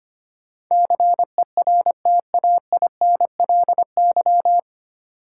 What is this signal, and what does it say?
Morse code, heard as a keyed tone, about 25 wpm, 700 Hz. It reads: CERTAINLY
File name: practice / 25wpm / 9 Letter Words - First 3 Letters Given - 25wpm